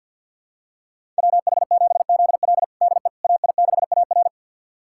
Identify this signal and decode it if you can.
Morse code, heard as a keyed tone, about 51 wpm, 700 Hz. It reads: WH7ZL DE AI6AR
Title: USA Callsign DE Callsign - 5 Characters - 50wpm